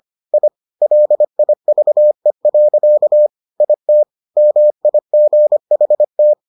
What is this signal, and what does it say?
Morse code, heard as a keyed tone, about 25 wpm, 600 Hz. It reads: I LIVE. IT MIGHT